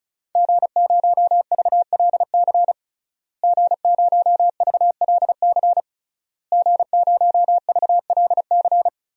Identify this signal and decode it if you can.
Morse code, heard as a keyed tone, about 36 wpm, 700 Hz. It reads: G0VLC G0VLC G0VLC
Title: DXCC-100 Call Signs - 5 Characters - Repeated 3x - 35wpm